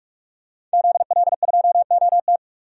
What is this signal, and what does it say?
Morse code, heard as a keyed tone, about 42 wpm, 700 Hz. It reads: ZL2OT
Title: DXCC-100 Call Signs - 5 Characters - 45wpm